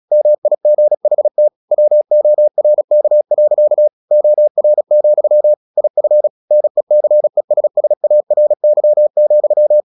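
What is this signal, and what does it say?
Morse code, heard as a keyed tone, about 36 wpm, 600 Hz. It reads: MIGHT WORK. OR, IF NECESSARY,
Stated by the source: Morse practice at 36 wpm, text of Ken Follett's novel Never, from CW med Gnister.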